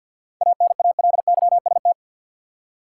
Morse code, heard as a keyed tone, about 52 wpm, 700 Hz. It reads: ANALYST
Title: Practice QSO - Exchange 3 - More Info - 50wpm